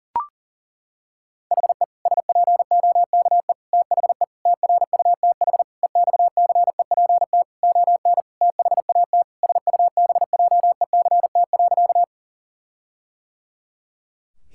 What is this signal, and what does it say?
Morse code, heard as a keyed tone, about 37 wpm, 700 Hz. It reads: HE SPOKE THE TRUTH EXCEPT ON THAT SUBJECT.
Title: Sentences from Top 1600 Words 40wpm